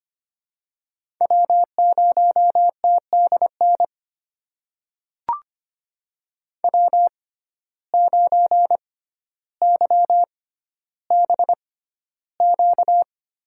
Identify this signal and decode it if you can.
Morse code, heard as a keyed tone, about 25 wpm, spaced as for 13 wpm, 700 Hz. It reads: W0TDN W9YBQ
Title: USA Call Signs - 6x Character Spacing - 25wpm